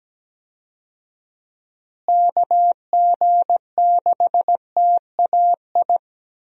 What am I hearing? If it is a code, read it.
Morse code, heard as a keyed tone, about 17 wpm, 700 Hz. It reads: KG6TAI
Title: USA Call Signs - 6 Characters - Repeated 2x - 17wpm